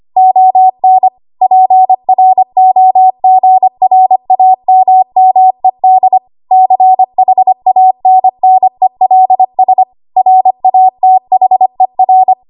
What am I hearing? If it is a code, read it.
Morse code, heard as a keyed tone, about 25 wpm, 750 Hz. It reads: ON PROGRAMMED CHANNELS RATHER
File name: Archive / 25 WPM / 25WPM